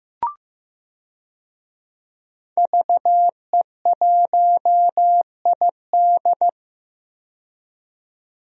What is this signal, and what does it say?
Morse code, heard as a keyed tone, about 15 wpm, 700 Hz. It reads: VE1ID
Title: DXCC-100 Call Signs - 5 Characters - 15wpm